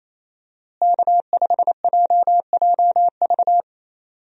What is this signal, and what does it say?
Morse code, heard as a keyed tone, about 28 wpm, 700 Hz. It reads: K5JJV
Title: USA Call Signs - 5 Characters - Repeated 2x - 28wpm